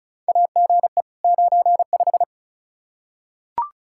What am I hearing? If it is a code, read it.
Morse code, heard as a keyed tone, about 36 wpm, 700 Hz. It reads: AGE 95
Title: K1USN SST QSO Practice 35wpm